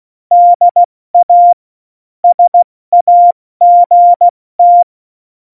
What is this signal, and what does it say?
Morse code, heard as a keyed tone, about 16 wpm, spaced as for 14 wpm, 700 Hz. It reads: DA SAGT
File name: Morsekurs / Witze / Witz 27. November 2018, 70 F80